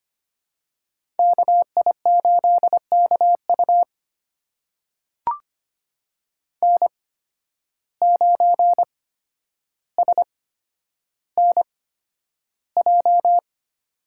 Morse code, heard as a keyed tone, about 25 wpm, spaced as for 11 wpm, 700 Hz. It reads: KI8KU N9SNJ
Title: USA Call Signs - 8x Character Spacing - 25wpm